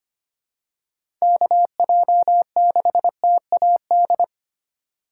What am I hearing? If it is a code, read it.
Morse code, heard as a keyed tone, about 25 wpm, 700 Hz. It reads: KJ6TAD
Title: USA Call Signs - 6 Characters - 2x Character Spacing - 25wpm